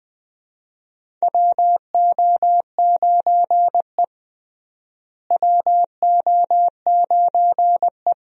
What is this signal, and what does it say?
Morse code, heard as a keyed tone, about 20 wpm, 700 Hz. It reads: WO9E WO9E
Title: USA Call Signs - 4 Characters - Repeated 2x - 20wpm